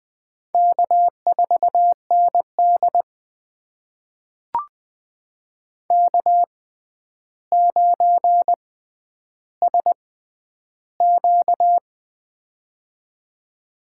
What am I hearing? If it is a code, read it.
Morse code, heard as a keyed tone, about 20 wpm, spaced as for 10 wpm, 700 Hz. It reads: K4ND K9SQ
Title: USA Call Signs - 4 Characters - 6x Character Spacing - 20wpm